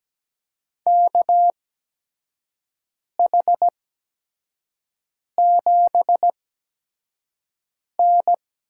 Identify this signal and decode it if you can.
Morse code, heard as a keyed tone, about 17 wpm, spaced as for 5 wpm, 700 Hz. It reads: KH7N
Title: USA Call Signs - 8x Character Spacing - 17wpm